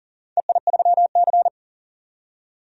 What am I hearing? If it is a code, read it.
Morse code, heard as a keyed tone, about 39 wpm, 700 Hz. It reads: EI3C
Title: DXCC-100 Call Signs - 4 Characters - 40wpm